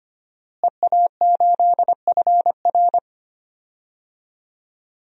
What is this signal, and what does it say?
Morse code, heard as a keyed tone, about 25 wpm, 700 Hz. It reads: EA8FR
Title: DXCC-100 Call Signs - 5 Characters - Repeated 3x - 25wpm